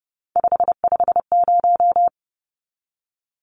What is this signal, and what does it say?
Morse code, heard as a keyed tone, about 30 wpm, 700 Hz. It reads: 550
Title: Numbers - 3-Digit 30wpm